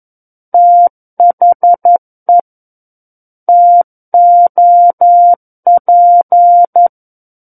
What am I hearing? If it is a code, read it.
Morse code, heard as a keyed tone, about 11 wpm, 700 Hz. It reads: THE TOP
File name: Book One / 02-Chapter 2 - 11 WPM